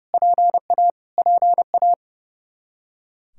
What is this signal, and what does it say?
Morse code, heard as a keyed tone, about 29 wpm, 700 Hz. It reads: PA PA
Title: Practice QSO - Exchange 1 - Beginning - 30wpm